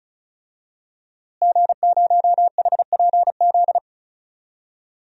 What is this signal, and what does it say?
Morse code, heard as a keyed tone, about 34 wpm, 700 Hz. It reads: G0HPZ
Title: DXCC-100 Call Signs - 5 Characters - Repeated 2x - 35wpm